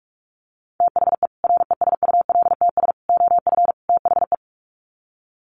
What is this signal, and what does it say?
Morse code, heard as a keyed tone, about 44 wpm, 700 Hz. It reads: THE RESULTS OF THE
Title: Sets of 4 Words 45wpm